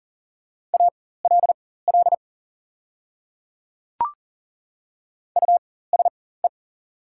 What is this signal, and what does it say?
Morse code, heard as a keyed tone, about 40 wpm, spaced as for 19 wpm, 690 Hz. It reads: ALL USE
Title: 4x Character Spacing - Top 100 Words - 40wpm